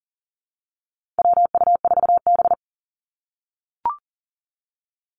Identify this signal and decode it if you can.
Morse code, heard as a keyed tone, about 40 wpm, 700 Hz. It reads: WU4B
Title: Call Signs - 40wpm